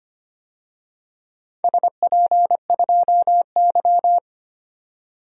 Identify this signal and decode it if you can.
Morse code, heard as a keyed tone, about 25 wpm, 700 Hz. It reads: SP2Y
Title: DXCC-100 Call Signs - 4 Characters - 25wpm